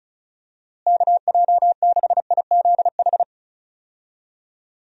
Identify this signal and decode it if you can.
Morse code, heard as a keyed tone, about 36 wpm, 700 Hz. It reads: KJ6IZH
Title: USA Call Signs - 6 Characters - Repeated 2x - 35wpm